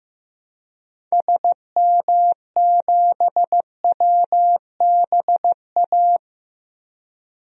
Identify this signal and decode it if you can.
Morse code, heard as a keyed tone, about 16 wpm, 700 Hz. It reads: SM7WBA
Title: DXCC-100 Call Signs - 6 Characters - 15wpm